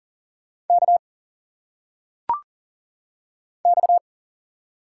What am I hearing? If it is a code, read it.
Morse code, heard as a keyed tone, about 40 wpm, 700 Hz. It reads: K X
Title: Single Letters - Phonetic Pronunciation 40wpm